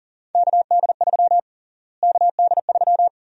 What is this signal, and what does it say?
Morse code, heard as a keyed tone, about 41 wpm, 700 Hz. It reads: KD3 KD3
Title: USA Call Sign Prefixes - Repeated 2x - 40wpm